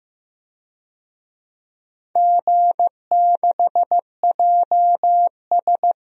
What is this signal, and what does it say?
Morse code, heard as a keyed tone, about 15 wpm, 700 Hz. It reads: G6JS
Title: DXCC-100 Call Signs - 4 Characters - 15wpm